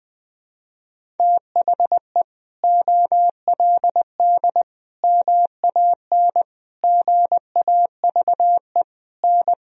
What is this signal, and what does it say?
Morse code, heard as a keyed tone, about 20 wpm, 700 Hz. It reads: THE OLD MAN GAVE N
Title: Sentences from Top 500 Words and Abbreviations 20wpm